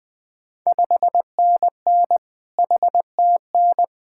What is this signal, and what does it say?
Morse code, heard as a keyed tone, about 20 wpm, 700 Hz. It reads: 5NN HTN
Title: Ontario QSO Party Exchange 20wpm